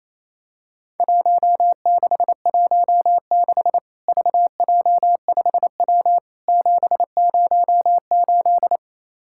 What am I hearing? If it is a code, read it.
Morse code, heard as a keyed tone, about 28 wpm, 700 Hz. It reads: 1616 VJ5W 708
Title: Worked All Europe DX Contest - QTC Exchange - 28wpm